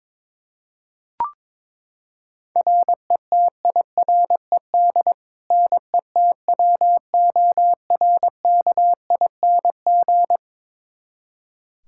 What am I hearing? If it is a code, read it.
Morse code, heard as a keyed tone, about 22 wpm, 700 Hz. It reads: RETIRED NETWORKING
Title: K1USN SST QSO Practice 22wpm